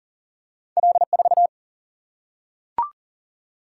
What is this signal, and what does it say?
Morse code, heard as a keyed tone, about 40 wpm, 700 Hz. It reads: L 4
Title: Single-Letter Single-Number 40wpm